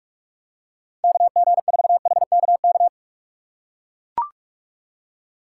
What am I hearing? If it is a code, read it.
Morse code, heard as a keyed tone, about 43 wpm, 700 Hz. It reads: KG4HKK